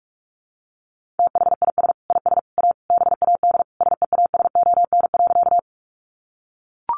Morse code, heard as a keyed tone, about 44 wpm, 700 Hz. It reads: THIS IS A BAD SEASON.
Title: Sentences from Top 1400 Words 45wpm